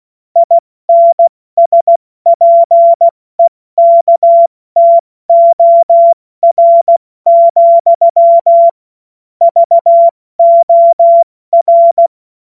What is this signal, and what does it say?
Morse code, heard as a keyed tone, about 16 wpm, 680 Hz. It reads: INSPEKTOR, VOR
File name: Morsekurs / Witze / Witz 19. April 2022 70 F80